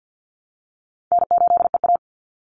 Morse code, heard as a keyed tone, about 49 wpm, 700 Hz. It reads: N8EA
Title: Call Signs Repeated 3x - 50wpm